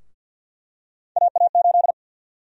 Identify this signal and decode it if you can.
Morse code, heard as a keyed tone, about 47 wpm, 700 Hz. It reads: AA8
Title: USA Call Sign Prefixes - 50wpm